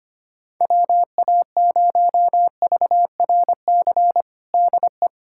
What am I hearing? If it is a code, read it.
Morse code, heard as a keyed tone, about 25 wpm, 700 Hz. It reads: WA0VRC DE
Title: USA Callsign DE Callsign - 6 Characters - 25wpm